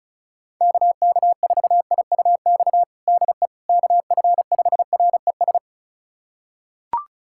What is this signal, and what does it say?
Morse code, heard as a keyed tone, about 36 wpm, 700 Hz. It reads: KK4IUX DE KF5RES
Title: USA Callsign DE Callsign 35wpm